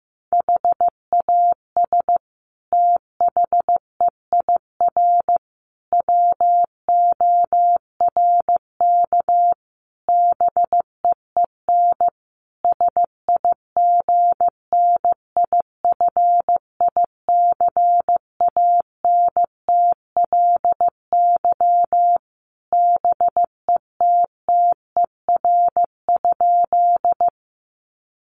Morse code, heard as a keyed tone, about 15 wpm, 700 Hz. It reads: HAS THEIR WORK BEEN SIGNIFICANTLY BETTER?
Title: Sentences from Top 1500 Words 15wpm